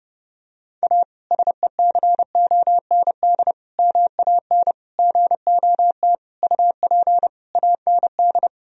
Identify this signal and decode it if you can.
Morse code, heard as a keyed tone, about 30 wpm, 700 Hz. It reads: A SECOND MAN GOT UP AND
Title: Sentences from Top 1000 Words - Review Part 1 - 30wpm